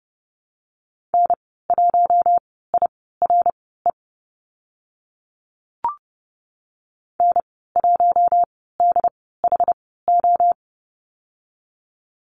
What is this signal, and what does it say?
Morse code, heard as a keyed tone, about 30 wpm, spaced as for 17 wpm, 700 Hz. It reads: N1IRE N1DHO
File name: Lesson 15g - Number 5 - Call Signs 30wpm